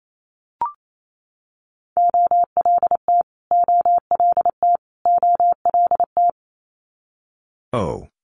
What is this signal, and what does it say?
Morse code, heard as a keyed tone, about 28 wpm, 700 Hz. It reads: OLT OLT OLT
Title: Multi-letter Phonemes Repeated 3x 28wpm